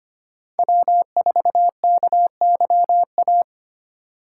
Morse code, heard as a keyed tone, about 25 wpm, 700 Hz. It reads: W4KYA
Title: USA Call Signs - 2x Character Spacing - 25wpm